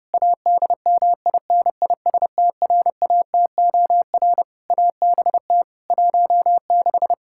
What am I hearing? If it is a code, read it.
Morse code, heard as a keyed tone, about 30 wpm, 700 Hz. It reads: ADMINISTRATOR ABT 16
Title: Practice QSO - Exchange 3 - More Info - 30wpm